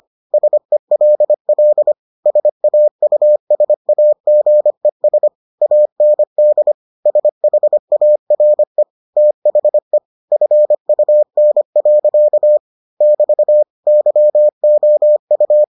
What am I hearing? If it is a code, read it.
Morse code, heard as a keyed tone, about 25 wpm, 600 Hz. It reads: SELL SAUSAGES AND SHARE THE FUN. <BT> YOU